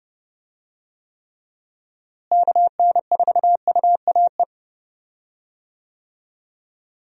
Morse code, heard as a keyed tone, about 32 wpm, 700 Hz. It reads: KN4UAE
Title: USA Call Signs - 6 Characters - Repeated 3x - 30wpm